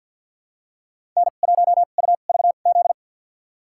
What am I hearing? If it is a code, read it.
Morse code, heard as a keyed tone, about 53 wpm, spaced as for 37 wpm, 690 Hz. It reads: N1UVZ
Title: USA Call Signs - 2x Character Spacing - 50wpm